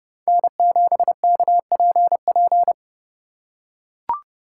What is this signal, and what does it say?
Morse code, heard as a keyed tone, about 30 wpm, 700 Hz. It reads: N7KPP
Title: USA Call Signs - 5 Characters - 4x Character Spacing - 30wpm